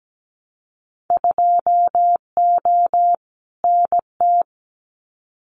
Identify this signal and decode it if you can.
Morse code, heard as a keyed tone, about 17 wpm, 700 Hz. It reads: 2O NT
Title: Winter Field Day Exchanges Encore 17wpm